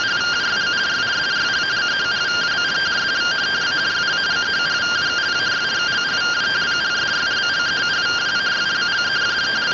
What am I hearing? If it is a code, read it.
Two-tone teleprinter signal, about 170 Hz shift.